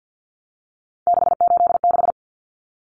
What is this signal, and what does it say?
Morse code, heard as a keyed tone, about 50 wpm, 700 Hz. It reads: 686
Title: Common Cut Numbers - 3-Digit 50wpm